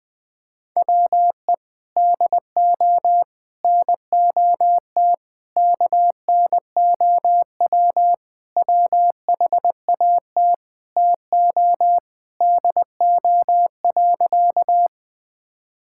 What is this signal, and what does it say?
Morse code, heard as a keyed tone, about 20 wpm, 700 Hz. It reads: WE DO NOT KNOW WHAT TO DO.
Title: Sentences with 5-Word Phrases - 20wpm